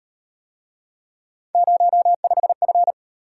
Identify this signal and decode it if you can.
Morse code, heard as a keyed tone, about 38 wpm, 700 Hz. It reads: 05F